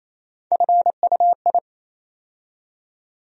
Morse code, heard as a keyed tone, about 28 wpm, 700 Hz. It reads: FUI
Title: USA Call Sign Postfixes - 3 Letters 28wpm